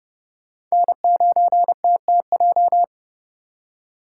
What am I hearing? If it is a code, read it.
Morse code, heard as a keyed tone, about 40 wpm, spaced as for 35 wpm, 700 Hz. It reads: N9TTJ
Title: Speed Racing - USA Call Signs 30wpm